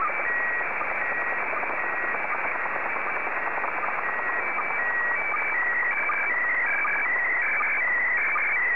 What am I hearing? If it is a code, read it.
SSTV (slow-scan television), PD 180.
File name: sstv / PD180-AX2024BLAH